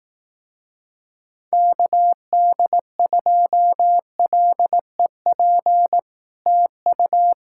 Morse code, heard as a keyed tone, about 18 wpm, 700 Hz. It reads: KD2LEP TU
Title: Speed-Racing Common QSO Phrases Encore - 15wpm